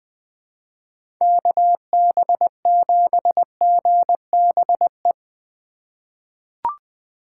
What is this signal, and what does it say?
Morse code, heard as a keyed tone, about 20 wpm, 700 Hz. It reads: KB7GBE
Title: USA Call Signs - 6 Characters - 4x Character Spacing - 20wpm